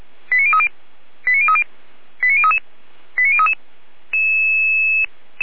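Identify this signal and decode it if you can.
Five-tone selective call (ZVEI), heard as quick runs of five tones, tones of about 70 ms. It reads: 89030 89030 89033 89033
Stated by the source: ZVEI five-tone selective call received with an SDR.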